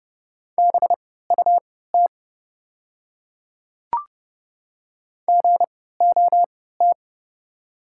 Morse code, heard as a keyed tone, about 30 wpm, spaced as for 17 wpm, 700 Hz. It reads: BUT GOT
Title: 3x Character Spacing - Top 300 Words - 30wpm